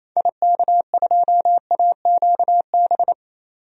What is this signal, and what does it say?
Morse code, heard as a keyed tone, about 28 wpm, 700 Hz. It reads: IK2AQB
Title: DXCC-100 Call Signs - 6 Characters - 28wpm